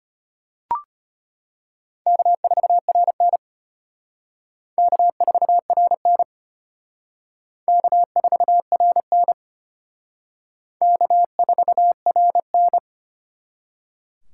Morse code, heard as a keyed tone, about 32 wpm, 700 Hz. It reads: K4RN K4RN K4RN K4RN